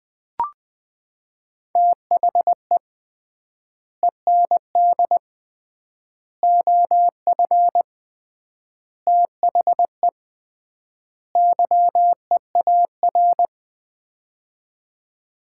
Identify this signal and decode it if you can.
Morse code, heard as a keyed tone, about 20 wpm, 700 Hz. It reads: THE END OF THE YEAR
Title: 3x Word-Spacing - Sets of 5 Words - 20wpm